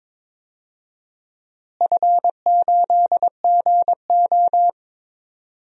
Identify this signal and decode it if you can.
Morse code, heard as a keyed tone, about 22 wpm, 700 Hz. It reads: F8GO
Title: DXCC-100 Call Signs - 4 Characters - Repeated 3x - 22wpm